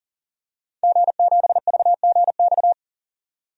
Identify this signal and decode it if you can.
Morse code, heard as a keyed tone, about 40 wpm, 700 Hz. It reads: G7VGX
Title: DXCC-100 Call Signs - 5 Characters - 40wpm